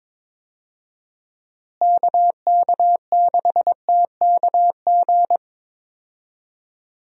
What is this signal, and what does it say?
Morse code, heard as a keyed tone, about 22 wpm, 700 Hz. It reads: KK6TKG
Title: USA Call Signs - 6 Characters - Repeated 3x - 22wpm